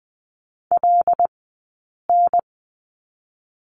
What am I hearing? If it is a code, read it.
Morse code, heard as a keyed tone, about 20 wpm, 700 Hz. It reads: L N
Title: 086 Lesson 15d - Number 5 - Two Characters 20wpm